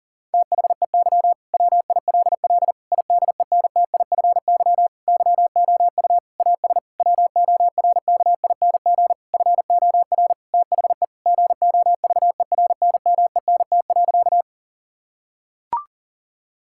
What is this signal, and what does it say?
Morse code, heard as a keyed tone, about 40 wpm, 700 Hz. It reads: THEY WILL IDENTIFY YOU AS WORKING FOR THE GOVERNMENT.
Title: Sentences from Top 1000 Words - Review Part 6 - 40wpm